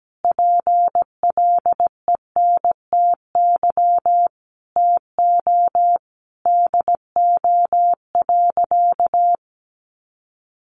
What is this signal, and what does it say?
Morse code, heard as a keyed tone, about 17 wpm, 700 Hz. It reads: PLENTY TO DO.
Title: Sentences from Top 1300 Words 17wpm